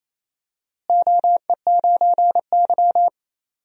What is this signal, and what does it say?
Morse code, heard as a keyed tone, about 28 wpm, 700 Hz. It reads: OE9Y
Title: DXCC-100 Call Signs - 4 Characters - Repeated 2x - 28wpm